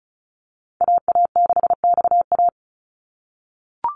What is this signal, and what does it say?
Morse code, heard as a keyed tone, about 36 wpm, 700 Hz. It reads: AA6XA
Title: NA SOTA Activator Callsigns 35wpm